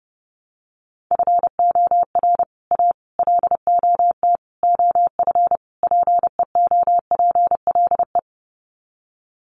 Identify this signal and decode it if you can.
Morse code, heard as a keyed tone, about 30 wpm, 700 Hz. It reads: FOR A LOT OF PEOPLE
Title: Sets of 5 Words Encore 30wpm